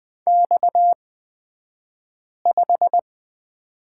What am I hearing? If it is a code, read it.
Morse code, heard as a keyed tone, about 20 wpm, spaced as for 5 wpm, 700 Hz. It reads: X5
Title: 228 Lesson 37d - Letter x - Two Characters 20wpm